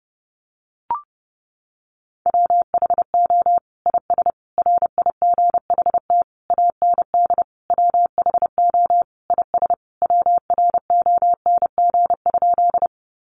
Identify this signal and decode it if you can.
Morse code, heard as a keyed tone, about 30 wpm, 700 Hz. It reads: WHO IS RIGHT AND WHO IS WRONG?
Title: Sentences from Top 900 Words 30wpm